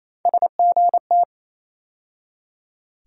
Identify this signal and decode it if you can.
Morse code, heard as a keyed tone, about 28 wpm, 700 Hz. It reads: SGT
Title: USA Call Sign Postfixes - 3 Letters 28wpm